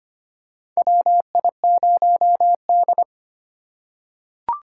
Morse code, heard as a keyed tone, about 25 wpm, 700 Hz. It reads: WI0D
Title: USA Call Signs - 4 Characters - 3x Character Spacing - 25wpm